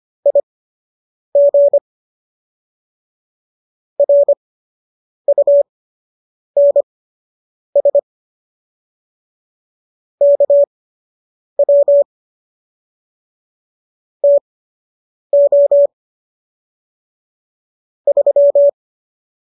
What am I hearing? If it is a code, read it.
Morse code, heard as a keyed tone, about 25 wpm, spaced as for 8 wpm, 570 Hz. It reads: IG RUNS KW TO 3